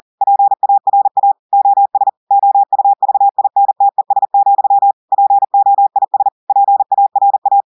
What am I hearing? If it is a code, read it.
Morse code, heard as a keyed tone, about 40 wpm, 800 Hz. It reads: PARA OS OUVINTES, POIS PARA
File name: audios / portugais / portugais texte en clair 2 vitesse 40 mots minutes